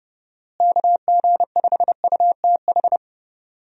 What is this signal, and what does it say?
Morse code, heard as a keyed tone, about 30 wpm, 700 Hz. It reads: KG5UTH
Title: USA Call Signs - 6 Characters - Repeated 3x - 30wpm